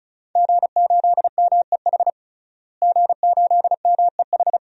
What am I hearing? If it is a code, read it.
Morse code, heard as a keyed tone, about 36 wpm, 700 Hz. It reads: G8MEH G8MEH
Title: DXCC-100 Call Signs - 5 Characters - Repeated 3x - 35wpm